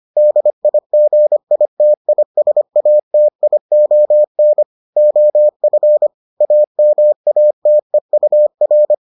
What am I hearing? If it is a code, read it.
Morse code, heard as a keyed tone, about 25 wpm, 600 Hz. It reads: DIGITISATION OF AMATEUR